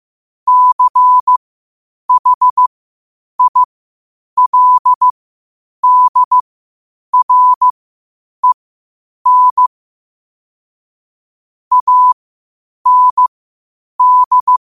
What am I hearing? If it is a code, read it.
Morse code, heard as a keyed tone, about 15 wpm, spaced as for 8 wpm, 1000 Hz. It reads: CHILDREN AND
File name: qotdfiles / 8wpm